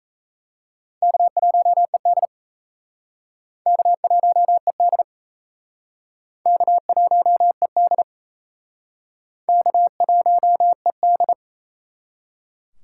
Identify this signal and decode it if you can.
Morse code, heard as a keyed tone, about 34 wpm, 700 Hz. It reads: K1ED K1ED K1ED K1ED
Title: Speed Racing - USA Call Signs 28wpm